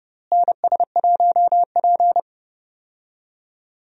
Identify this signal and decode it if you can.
Morse code, heard as a keyed tone, about 30 wpm, 700 Hz. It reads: NS1P
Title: USA Call Signs - 4 Characters - Repeated 3x - 30wpm